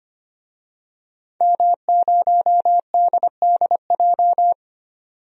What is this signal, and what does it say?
Morse code, heard as a keyed tone, about 25 wpm, 700 Hz. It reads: M0DDJ
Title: DXCC-100 Call Signs - 5 Characters - 25wpm